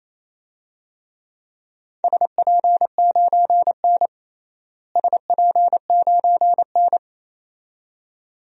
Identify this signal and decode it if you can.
Morse code, heard as a keyed tone, about 28 wpm, 700 Hz. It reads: SP9N SP9N
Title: DXCC-100 Call Signs - 4 Characters - Repeated 2x - 28wpm